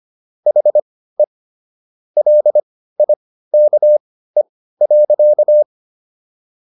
Morse code, heard as a keyed tone, about 25 wpm, spaced as for 15 wpm, 600 Hz. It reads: HE LIKE.